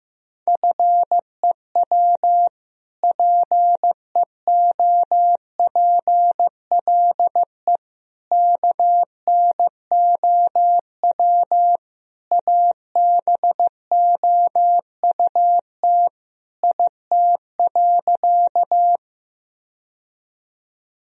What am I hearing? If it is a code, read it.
Morse code, heard as a keyed tone, about 15 wpm, 700 Hz. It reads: FEW PEOPLE KNOW ABOUT IT.
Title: Sentences from Top 1000 Words - Review Part 1 - 15wpm